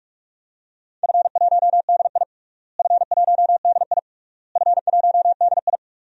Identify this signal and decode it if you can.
Morse code, heard as a keyed tone, about 45 wpm, 700 Hz. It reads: F1DI F1DI F1DI
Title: DXCC-100 Call Signs - 4 Characters - Repeated 3x - 45wpm